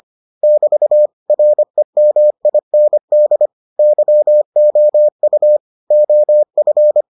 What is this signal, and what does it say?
Morse code, heard as a keyed tone, about 25 wpm, 600 Hz. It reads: <BT> REMIND YOU OF